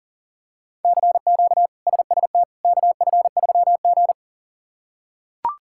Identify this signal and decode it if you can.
Morse code, heard as a keyed tone, about 40 wpm, 700 Hz. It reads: CQ SST KF3G
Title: K1USN SST QSO Practice - 40wpm